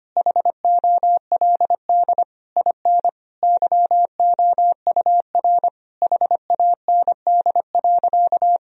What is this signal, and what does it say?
Morse code, heard as a keyed tone, about 25 wpm, 700 Hz. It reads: HOLD IN YOUR HAND.
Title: Sentences from Top 1000 Words - Review Part 2 - 25wpm